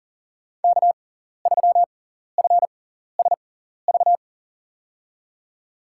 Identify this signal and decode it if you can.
Morse code, heard as a keyed tone, about 37 wpm, spaced as for 14 wpm, 700 Hz. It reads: K3FSV
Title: USA Call Signs - 6x Character Spacing - 40wpm